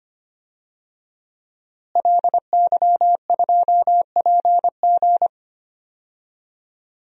Morse code, heard as a keyed tone, about 25 wpm, 700 Hz. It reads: LY2PG
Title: DXCC-100 Call Signs - 5 Characters - Repeated 2x - 25wpm